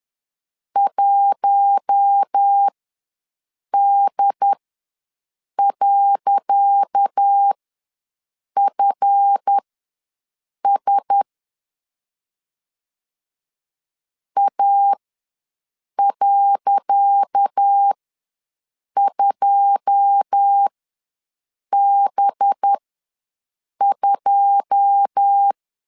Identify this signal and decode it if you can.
Morse code, heard as a keyed tone, about 10 wpm, spaced as for 6 wpm, 800 Hz. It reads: 1D.FS A.2B2
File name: LEZIONI / 10 SIGNS / 10C DOT ws 10wpm 4213